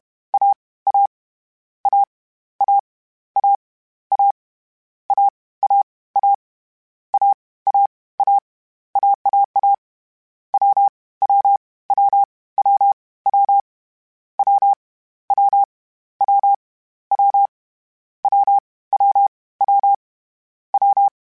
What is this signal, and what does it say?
Morse code, heard as a keyed tone, about 31 wpm, 800 Hz. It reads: A A A A A A A A A A A A AAA W W W W W W W W W W W W W
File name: Difficult Combos / Difficult Combos-mp3s / AWJ1 @30